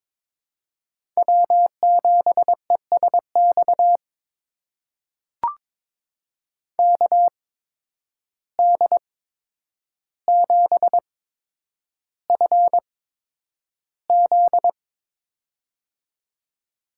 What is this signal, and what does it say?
Morse code, heard as a keyed tone, about 23 wpm, spaced as for 10 wpm, 700 Hz. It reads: W7ESX KD7FZ
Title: USA Call Signs - 5 Characters - 8x Character Spacing - 22wpm